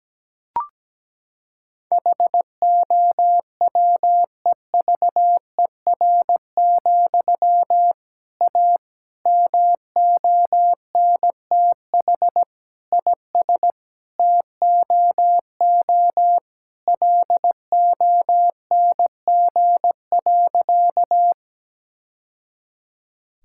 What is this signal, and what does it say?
Morse code, heard as a keyed tone, about 17 wpm, 700 Hz. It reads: HOWEVER, A MONTH IS TOO LONG.
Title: Sentences from Top 1000 Words - Review Part 5 - 17wpm